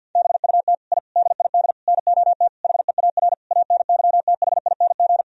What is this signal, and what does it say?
Morse code, heard as a keyed tone, about 50 wpm, 700 Hz. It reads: BUT I DID NOT HEAR ANYTHING
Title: Sentences from Top 1000 Words - Review Part 4 - 50wpm